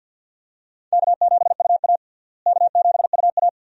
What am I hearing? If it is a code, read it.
Morse code, heard as a keyed tone, about 48 wpm, 700 Hz. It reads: K7UA K7UA
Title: USA Call Signs - 4 Characters - Repeated 2x - 50wpm